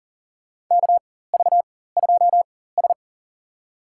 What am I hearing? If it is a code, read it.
Morse code, heard as a keyed tone, about 40 wpm, spaced as for 19 wpm, 700 Hz. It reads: KV2S
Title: USA Call Signs - 4 Characters - 4x Character Spacing - 40wpm